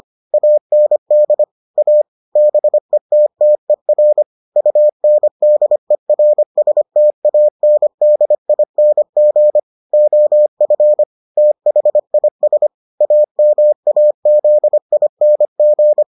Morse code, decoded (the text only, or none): AND A BETTER UNDERSTANDING OF THIS AMAZING